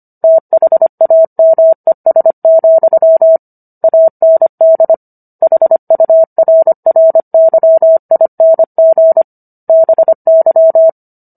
Morse code, heard as a keyed tone, about 25 wpm, 650 Hz. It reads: THAMES, AND HURRYING BY